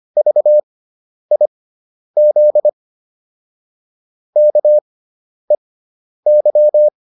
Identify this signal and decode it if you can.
Morse code, heard as a keyed tone, about 25 wpm, spaced as for 10 wpm, 600 Hz. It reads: VIZ KEY